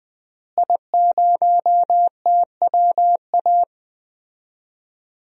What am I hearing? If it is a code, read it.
Morse code, heard as a keyed tone, about 21 wpm, 700 Hz. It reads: I0TWA
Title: DXCC-100 Call Signs - 5 Characters - Repeated 3x - 20wpm